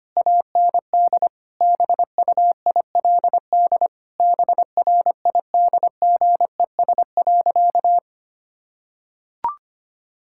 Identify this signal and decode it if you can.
Morse code, heard as a keyed tone, about 25 wpm, 700 Hz. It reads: AND BUILD BRIDGES.
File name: practice / 25wpm / Difficult Sentences with Hints - 25wpm